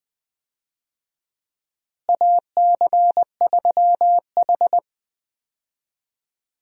Morse code, heard as a keyed tone, about 21 wpm, 700 Hz. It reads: AC3H
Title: USA Call Signs - 4 Characters - Repeated 2x - 20wpm